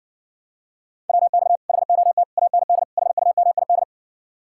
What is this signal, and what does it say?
Morse code, heard as a keyed tone, about 60 wpm, 700 Hz. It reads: WX HOT AND HUMID